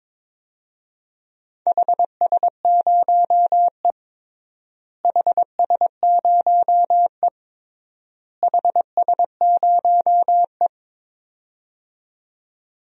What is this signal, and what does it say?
Morse code, heard as a keyed tone, about 22 wpm, 700 Hz. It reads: HS0E HS0E HS0E
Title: DXCC-100 Call Signs - 4 Characters - Repeated 3x - 22wpm